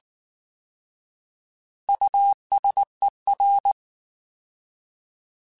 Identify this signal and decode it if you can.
Morse code, heard as a keyed tone, about 19 wpm, 800 Hz. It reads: USER